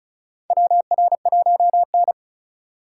Morse code, heard as a keyed tone, about 34 wpm, 700 Hz. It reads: WR1N
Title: USA Call Signs - 4x Character Spacing - 35wpm